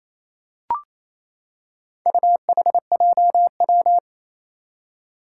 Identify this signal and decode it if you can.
Morse code, heard as a keyed tone, about 27 wpm, 700 Hz. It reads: UHJW
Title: Four Challenging Letter-Number Combinations 28wpm